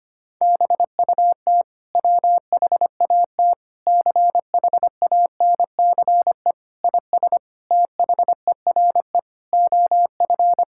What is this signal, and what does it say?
Morse code, heard as a keyed tone, about 25 wpm, 700 Hz. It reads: BUT WHAT CHANCE IS THERE OF
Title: Sentences from Top 1000 Words - Review Part 6 - 25wpm